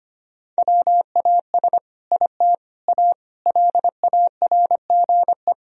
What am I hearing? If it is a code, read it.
Morse code, heard as a keyed tone, about 25 wpm, 700 Hz. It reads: WAS IT A LARGE
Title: Sentences from Top 1000 Words - Review Part 4 - 25wpm